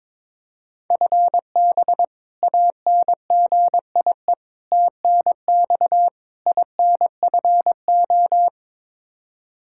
Morse code, heard as a keyed tone, about 22 wpm, 700 Hz. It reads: FB ANGIE TNX INFO
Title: K1USN SST QSO Practice 22wpm